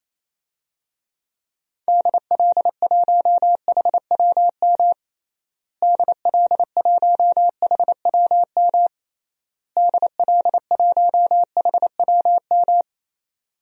Morse code, heard as a keyed tone, about 28 wpm, 700 Hz. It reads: DL1HWM DL1HWM DL1HWM
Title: DXCC-100 Call Signs - 6 Characters - Repeated 3x - 28wpm